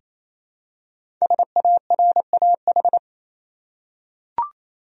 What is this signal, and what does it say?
Morse code, heard as a keyed tone, about 27 wpm, 700 Hz. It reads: SARAH
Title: USA Common Names - Max 5 Letters - 28wpm